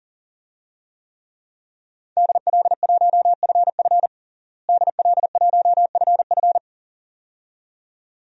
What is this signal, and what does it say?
Morse code, heard as a keyed tone, about 40 wpm, 700 Hz. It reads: DL1FF DL1FF
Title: DXCC-100 Call Signs - 5 Characters - Repeated 2x - 40wpm